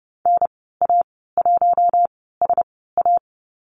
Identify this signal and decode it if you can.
Morse code, heard as a keyed tone, about 30 wpm, spaced as for 17 wpm, 700 Hz. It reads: NA1SA
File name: Lesson 15g - Number 5 - Call Signs 30wpm